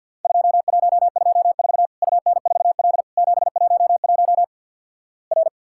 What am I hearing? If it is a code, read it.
Morse code, heard as a keyed tone, about 52 wpm, 700 Hz. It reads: 2124 UN4L 711 R
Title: Worked All Europe DX Contest - QTC Exchange with Repeats - 50wpm